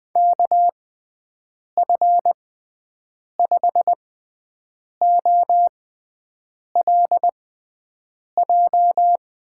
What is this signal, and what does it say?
Morse code, heard as a keyed tone, about 20 wpm, spaced as for 7 wpm, 700 Hz. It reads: KF5OLJ
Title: USA Call Signs - 6 Characters - 6x Character Spacing - 20wpm